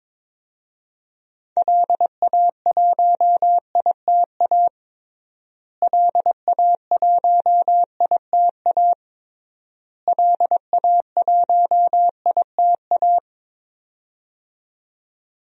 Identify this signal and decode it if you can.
Morse code, heard as a keyed tone, about 22 wpm, 700 Hz. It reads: LA1ITA LA1ITA LA1ITA
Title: DXCC-100 Call Signs - 6 Characters - Repeated 3x - 22wpm